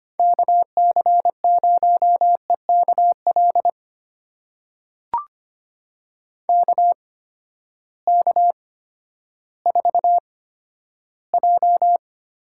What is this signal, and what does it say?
Morse code, heard as a keyed tone, about 25 wpm, spaced as for 12 wpm, 700 Hz. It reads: KC0EKL KK4J
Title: USA Call Signs - 6 Characters - 8x Character Spacing - 25wpm